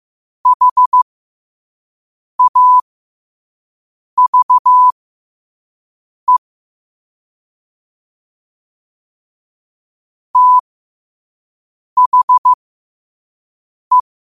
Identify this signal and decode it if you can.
Morse code, heard as a keyed tone, about 15 wpm, spaced as for 5 wpm, 1000 Hz. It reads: HAVE THE
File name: qotdfiles / 5wpm